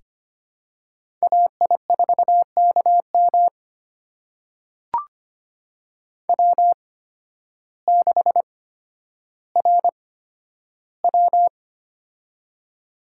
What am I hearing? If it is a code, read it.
Morse code, heard as a keyed tone, about 25 wpm, spaced as for 12 wpm, 700 Hz. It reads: AI4KM W6RW
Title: USA Call Signs - 8x Character Spacing - 25wpm